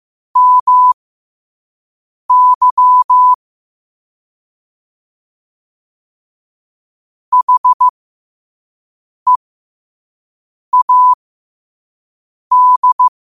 Morse code, heard as a keyed tone, about 15 wpm, spaced as for 5 wpm, 1000 Hz. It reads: MY HEAD
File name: qotdfiles / 5wpm